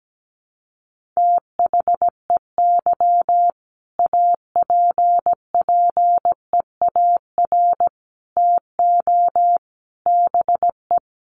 Morse code, heard as a keyed tone, about 17 wpm, 700 Hz. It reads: THEY APPEAR TO BE
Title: Sentences from Top 1400 Words 17wpm